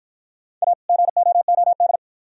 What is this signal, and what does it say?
Morse code, heard as a keyed tone, about 51 wpm, 700 Hz. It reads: A GOOD